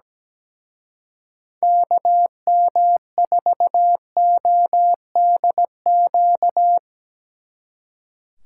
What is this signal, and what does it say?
Morse code, heard as a keyed tone, about 19 wpm, 700 Hz. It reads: KM4ODQ
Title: Speed Racing - USA 2x3 Call Signs 17wpm